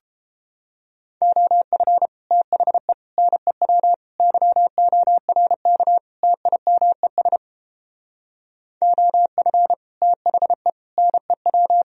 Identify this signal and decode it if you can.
Morse code, heard as a keyed tone, about 29 wpm, 700 Hz. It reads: OF THE NEW YORK TIMES OF THE NEW